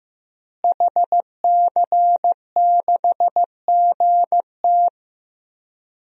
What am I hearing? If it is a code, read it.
Morse code, heard as a keyed tone, about 15 wpm, 700 Hz. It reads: HC6GT
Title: DXCC-100 Call Signs - 5 Characters - Repeated 3x - 15wpm